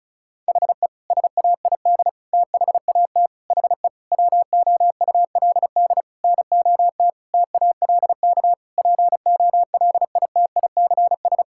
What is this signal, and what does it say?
Morse code, heard as a keyed tone, about 34 wpm, 700 Hz. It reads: HE SAID THAT HE WOULD NOT TALK POLITICS
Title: Sentences from Top 1000 Words - Review Part 6 - 35wpm